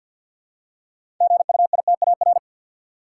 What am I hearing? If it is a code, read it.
Morse code, heard as a keyed tone, about 53 wpm, 700 Hz. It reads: GUITAR
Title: Six Letter Words - 50wpm